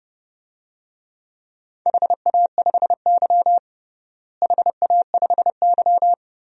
Morse code, heard as a keyed tone, about 30 wpm, 700 Hz. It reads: HA5Y HA5Y
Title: DXCC-100 Call Signs - 4 Characters - Repeated 3x - 30wpm